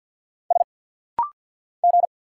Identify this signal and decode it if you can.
Morse code, heard as a keyed tone, about 50 wpm, 700 Hz. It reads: S G
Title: Single Letters - Rapid-Fire 50wpm